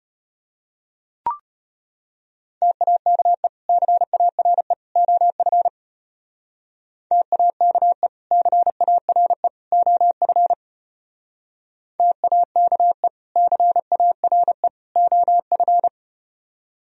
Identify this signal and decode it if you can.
Morse code, heard as a keyed tone, about 34 wpm, 700 Hz. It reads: TAKE CARE OF TAKE CARE OF TAKE CARE OF